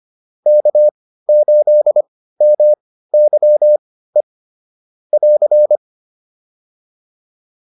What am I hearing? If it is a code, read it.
Morse code, heard as a keyed tone, about 25 wpm, spaced as for 15 wpm, 600 Hz. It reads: K8MYE <AR>
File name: QSO403 15